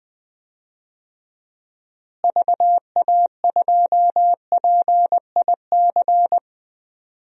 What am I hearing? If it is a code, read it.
Morse code, heard as a keyed tone, about 20 wpm, 700 Hz. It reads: VA2PIC
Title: DXCC-100 Call Signs - 6 Characters - Repeated 3x - 20wpm